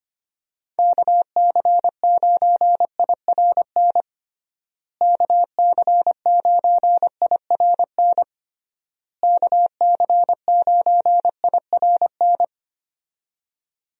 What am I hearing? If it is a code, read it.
Morse code, heard as a keyed tone, about 25 wpm, 700 Hz. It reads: KC9IRN KC9IRN KC9IRN
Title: USA Call Signs - 6 Characters - Repeated 3x - 25wpm